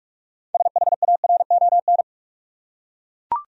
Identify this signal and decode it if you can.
Morse code, heard as a keyed tone, about 42 wpm, 700 Hz. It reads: SHARON
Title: USA Common Names - 45wpm